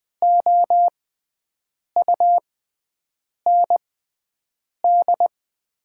Morse code, heard as a keyed tone, about 20 wpm, spaced as for 7 wpm, 690 Hz. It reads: OUND
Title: 6x Character Spacing - Top 300 Words - 20wpm